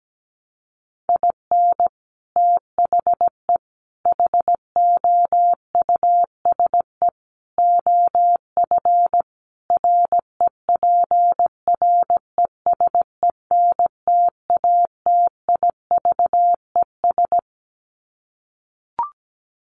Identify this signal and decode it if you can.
Morse code, heard as a keyed tone, about 17 wpm, 700 Hz. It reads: IN THE HOUSE OF REPRESENTATIVES
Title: Sets of 5 Words Encore 17wpm